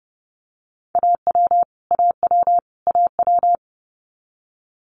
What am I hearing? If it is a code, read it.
Morse code, heard as a keyed tone, about 31 wpm, 700 Hz. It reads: AW AW AW
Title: Multi-letter Phonemes Repeated 3x 30wpm